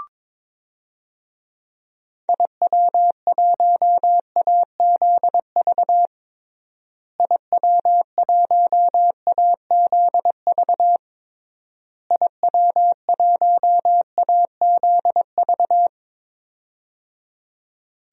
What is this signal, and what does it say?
Morse code, heard as a keyed tone, about 22 wpm, 700 Hz. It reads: IW1AZV IW1AZV IW1AZV
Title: DXCC-100 Call Signs - 6 Characters - Repeated 3x - 22wpm